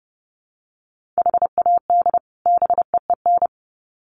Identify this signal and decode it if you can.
Morse code, heard as a keyed tone, about 30 wpm, 700 Hz. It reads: HAD BEEN
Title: Sets of 2 Words 30wpm